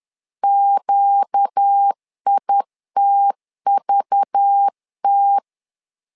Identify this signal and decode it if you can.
Morse code, heard as a keyed tone, about 10 wpm, 800 Hz. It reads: QITVT
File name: LEZIONI / 5 LBVC / 5 LBV 10wpm 2248